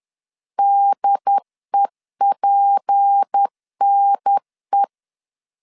Morse code, heard as a keyed tone, about 10 wpm, 800 Hz. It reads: DEPNE
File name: CW / DET / 1 6 1500C S B 10wpm 4137